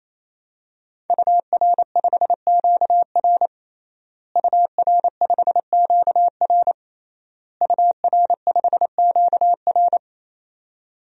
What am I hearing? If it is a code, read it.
Morse code, heard as a keyed tone, about 27 wpm, 700 Hz. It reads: UR5QR UR5QR UR5QR
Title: DXCC-100 Call Signs - 5 Characters - Repeated 3x - 28wpm